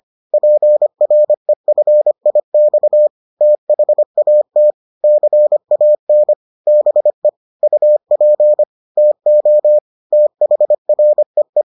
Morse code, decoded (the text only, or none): PREFIX THAT CAN BE UP TO THREE